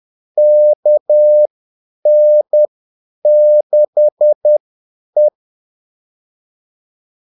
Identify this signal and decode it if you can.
Morse code, heard as a keyed tone, about 10 wpm, spaced as for 8 wpm, 600 Hz. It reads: KN6E
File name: Callsigns-10-8wpm